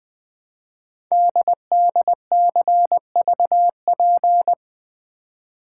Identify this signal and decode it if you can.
Morse code, heard as a keyed tone, about 20 wpm, 700 Hz. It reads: DDCVP
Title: Code Groups - 5 Letters - 20wpm